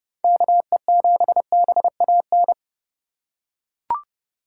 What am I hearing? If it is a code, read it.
Morse code, heard as a keyed tone, about 30 wpm, 700 Hz. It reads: KE7BAN